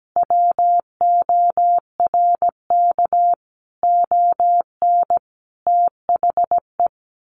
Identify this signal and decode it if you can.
Morse code, heard as a keyed tone, about 17 wpm, 700 Hz. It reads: WORK ON THE
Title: Sentences from Top 1200 Words 17wpm